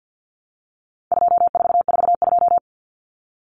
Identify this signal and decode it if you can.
Morse code, heard as a keyed tone, about 50 wpm, 700 Hz. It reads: 2442
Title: Numbers - 4-Digit 50wpm